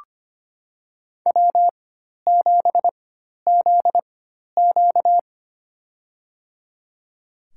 Morse code, heard as a keyed tone, about 25 wpm, spaced as for 12 wpm, 700 Hz. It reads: W7ZQ
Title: USA Call Signs - 4 Characters - 4x Character Spacing - 25wpm